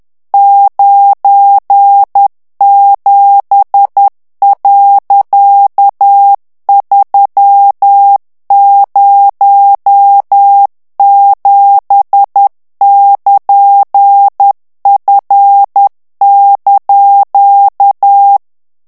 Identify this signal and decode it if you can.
Morse code, heard as a keyed tone, about 10 wpm, 800 Hz. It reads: 97.307<KN>F)
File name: ANGLAIS texte clair 3 vitesse 10 mots minute